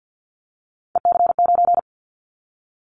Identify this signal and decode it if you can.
Morse code, heard as a keyed tone, about 50 wpm, 700 Hz. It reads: EC9
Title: Callsign Prefixes for Top 100 DXCC - 50wpm